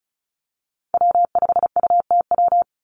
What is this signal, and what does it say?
Morse code, heard as a keyed tone, about 34 wpm, 700 Hz. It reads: W5UTW
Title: Call Signs Repeated 2x - 35wpm